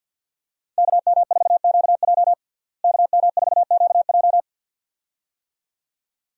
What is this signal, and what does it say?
Morse code, heard as a keyed tone, about 52 wpm, 700 Hz. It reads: KM4QJ KM4QJ
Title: USA Call Signs - 5 Characters - Repeated 2x - 50wpm